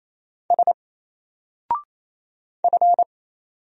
Morse code, heard as a keyed tone, about 27 wpm, 700 Hz. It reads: S F
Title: Single Character (Letter:Number:Punctuation:Prosign) 28wpm